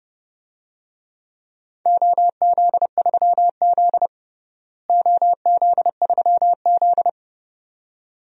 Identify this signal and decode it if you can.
Morse code, heard as a keyed tone, about 29 wpm, 700 Hz. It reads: OZ3Z OZ3Z
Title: DXCC-100 Call Signs - 4 Characters - Repeated 2x - 30wpm